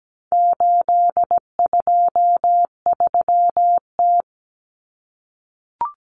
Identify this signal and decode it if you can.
Morse code, heard as a keyed tone, about 17 wpm, 700 Hz. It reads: 823T
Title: Common Cut Numbers - 4-Digit 17wpm